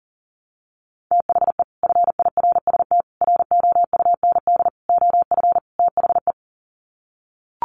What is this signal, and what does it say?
Morse code, heard as a keyed tone, about 40 wpm, 700 Hz. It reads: THE FIRST ROUND OF THE
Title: Sets of 5 Words Encore 40wpm